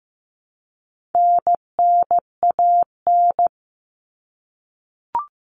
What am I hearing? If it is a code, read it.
Morse code, heard as a keyed tone, about 15 wpm, 700 Hz. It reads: NNAN
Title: All Cut Numbers - 4-Digit 15wpm